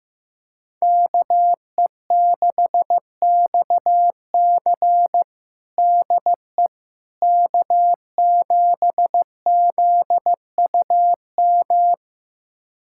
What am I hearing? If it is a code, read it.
Morse code, heard as a keyed tone, about 15 wpm, 700 Hz. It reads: KE6XC DE K7ZUM
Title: USA Callsign DE Callsign - 5 Characters - 15wpm